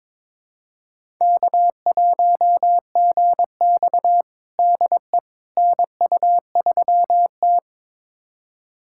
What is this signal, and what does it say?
Morse code, heard as a keyed tone, about 22 wpm, 700 Hz. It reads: K1GX DE NU3T
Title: USA Callsign DE Callsign - 4 Characters - 22wpm